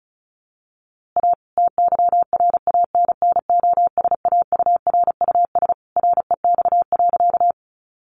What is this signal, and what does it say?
Morse code, heard as a keyed tone, about 34 wpm, 700 Hz. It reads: A TYRANNOSAURUS REX.
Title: A Sound Of Thunder 35wpm